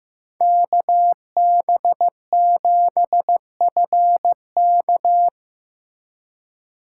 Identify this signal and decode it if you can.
Morse code, heard as a keyed tone, about 15 wpm, 700 Hz. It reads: KB7FK
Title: USA Call Signs - 5 Characters - 2x Character Spacing - 15wpm